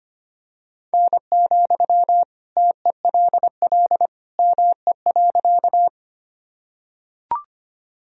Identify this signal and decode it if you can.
Morse code, heard as a keyed tone, about 25 wpm, 700 Hz. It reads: N, TELL ME.
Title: Sentences from Top 200 Words and Abbreviations 25wpm